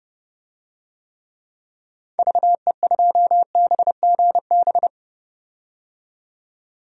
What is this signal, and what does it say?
Morse code, heard as a keyed tone, about 30 wpm, 700 Hz. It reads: VE2BGB
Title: DXCC-100 Call Signs - 6 Characters - 30wpm